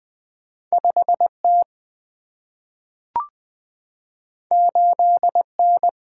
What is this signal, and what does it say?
Morse code, heard as a keyed tone, about 20 wpm, 700 Hz. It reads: 5T 8N
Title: Single-Letter Single-Number 20wpm